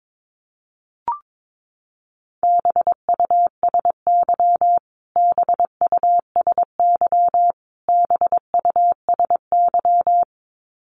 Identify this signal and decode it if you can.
Morse code, heard as a keyed tone, about 22 wpm, 700 Hz. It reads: BUSY BUSY BUSY
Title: Top 500 Words Repeated 22wpm